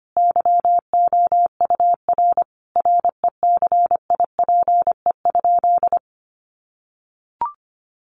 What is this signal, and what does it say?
Morse code, heard as a keyed tone, about 25 wpm, 700 Hz. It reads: YOUR RECIPE?
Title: Sentences from Top 1200 Words 25wpm